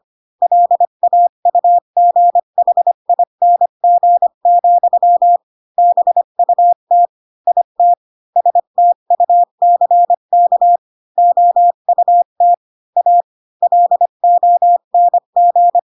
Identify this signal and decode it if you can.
Morse code, heard as a keyed tone, about 25 wpm, 700 Hz. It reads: LAUGHING, BUT IT STUCK OUT A LONG
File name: Pinocchio / Pinocchio ch3 20 25 ik1hge